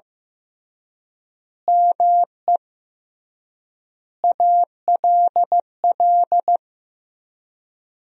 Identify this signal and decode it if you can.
Morse code, heard as a keyed tone, about 15 wpm, 700 Hz. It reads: ME ALL
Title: 3x Word-Spacing - Sentences from Top 200 Words - 15wpm